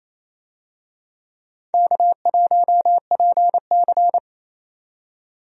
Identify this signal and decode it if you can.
Morse code, heard as a keyed tone, about 27 wpm, 700 Hz. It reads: K1PC
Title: USA Call Signs - 4 Characters - 28wpm